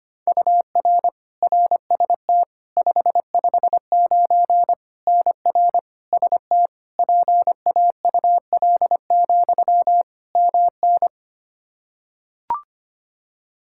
Morse code, heard as a keyed tone, about 25 wpm, 700 Hz. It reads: UR RST 559 NR ST PAUL, MN
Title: Practice QSO - Exchange 1 - Beginning - 25wpm